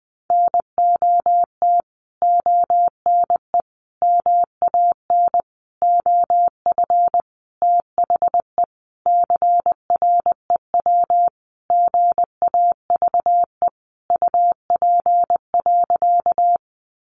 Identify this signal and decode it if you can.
Morse code, heard as a keyed tone, about 20 wpm, 700 Hz. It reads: NOT ONE MAN OF THE CREW GAVE UP.
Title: Sentences from Top 1500 Words 20wpm